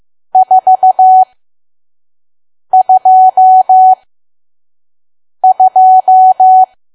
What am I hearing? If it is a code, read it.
Morse code, heard as a keyed tone, about 15 wpm, spaced as for 5 wpm, 750 Hz. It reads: 422
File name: Archive / 5 WPM files / PM